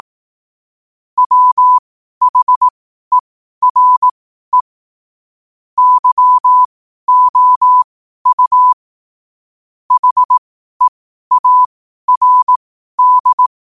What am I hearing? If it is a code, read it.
Morse code, heard as a keyed tone, about 18 wpm, spaced as for 12 wpm, 1000 Hz. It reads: WHERE YOU HEARD
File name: qotdfiles / 12wpm